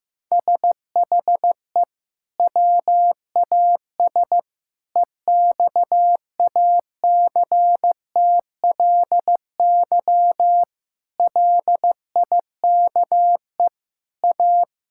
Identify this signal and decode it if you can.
Morse code, heard as a keyed tone, about 16 wpm, 700 Hz. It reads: SHE WAS EXACTLY LIKE A
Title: Sentences from Top 1000 Words - Review Part 6 - 15wpm